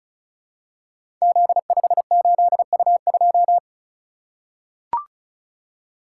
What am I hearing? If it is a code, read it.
Morse code, heard as a keyed tone, about 36 wpm, 700 Hz. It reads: Z58U2
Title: Code Groups - 5 Letter:Numbers - 35wpm